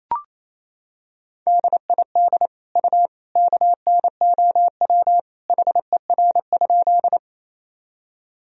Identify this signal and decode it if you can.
Morse code, heard as a keyed tone, about 28 wpm, 700 Hz. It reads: DID U KNOW HER?
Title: Sentences from Top 100 Words and Abbreviations 28wpm